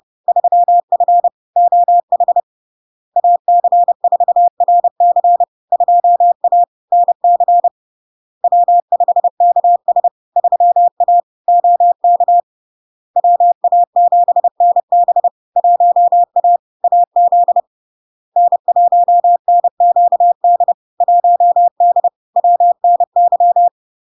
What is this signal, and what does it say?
Morse code, heard as a keyed tone, about 30 wpm, 700 Hz. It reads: 3F OH AC4RC 2A NC W5KS 3A OK WA7NB 1A AZ N1NQD 1D WNY
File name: LARK-FD-2022-Log-in-CW-30wpm-700-Hz